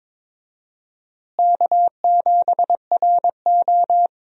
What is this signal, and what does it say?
Morse code, heard as a keyed tone, about 22 wpm, 700 Hz. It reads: K7RO
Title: USA Call Signs - 4 Characters - Repeated 2x - 22wpm